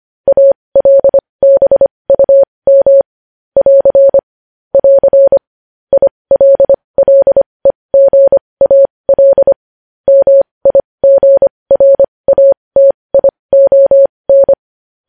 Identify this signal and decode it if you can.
Morse code, heard as a keyed tone, about 25 wpm, spaced as for 20 wpm, 550 Hz. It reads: ALBUM <AR> <AR> ILLEGAL MIGRATION
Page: news headlines in morse code at 20 wpm